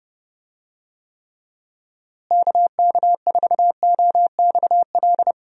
Morse code, heard as a keyed tone, about 30 wpm, 700 Hz. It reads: KK4OXL
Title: USA Call Signs - 6 Characters - Repeated 2x - 30wpm